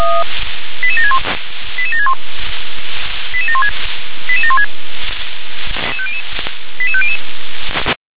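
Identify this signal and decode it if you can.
Five-tone selective call (ZVEI), heard as quick runs of five tones, tones of about 70 ms. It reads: 88761 88761 88716 88716 88500
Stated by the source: ZVEI five-tone selective call received by SDR.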